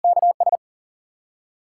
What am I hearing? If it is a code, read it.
Morse code, heard as a keyed tone, about 52 wpm, 700 Hz. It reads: K S